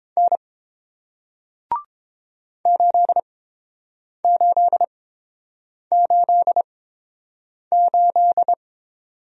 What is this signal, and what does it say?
Morse code, heard as a keyed tone, about 30 wpm, spaced as for 7 wpm, 700 Hz. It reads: N 8888